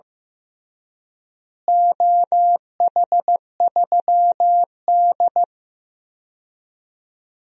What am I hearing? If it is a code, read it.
Morse code, heard as a keyed tone, about 16 wpm, 700 Hz. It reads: OH3D
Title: DXCC-100 Call Signs - 4 Characters - Repeated 3x - 15wpm